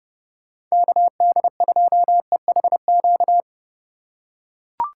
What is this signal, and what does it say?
Morse code, heard as a keyed tone, about 30 wpm, 700 Hz. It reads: KD2EHQ
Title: USA Call Signs - 6 Characters - 3x Character Spacing - 30wpm